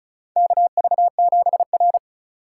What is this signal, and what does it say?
Morse code, heard as a keyed tone, about 36 wpm, 700 Hz. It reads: KV7R
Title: USA Call Signs - 4 Characters - 2x Character Spacing - 35wpm